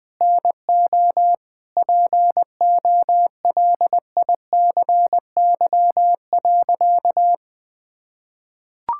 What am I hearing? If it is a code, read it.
Morse code, heard as a keyed tone, about 20 wpm, 700 Hz. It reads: NO POLICY.
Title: Sentences from Top 1000 Words - Review Part 5 - 20wpm